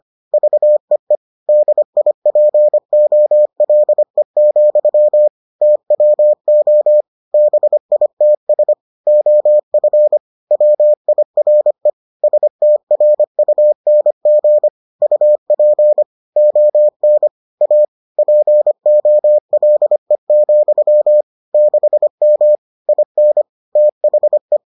Morse code, heard as a keyed tone, about 25 wpm, 600 Hz. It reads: VEE DIPOLE, TWO BITS OF WIRE STRUNG UP ON A POLE, 6M IN THE